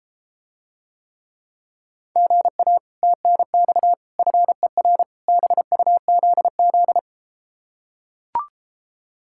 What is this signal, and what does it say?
Morse code, heard as a keyed tone, about 33 wpm, 700 Hz. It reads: GA TNX FER BUZZ